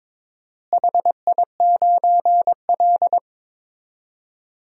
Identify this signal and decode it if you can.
Morse code, heard as a keyed tone, about 22 wpm, 700 Hz. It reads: HI9L
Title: DXCC-100 Call Signs - 4 Characters - Repeated 2x - 22wpm